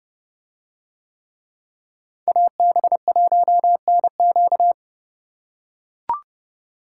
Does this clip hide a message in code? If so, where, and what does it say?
Yes: Morse code from 2.3 to 4.7 s, heard as a keyed tone, about 28 wpm, 700 Hz. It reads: AB1NQ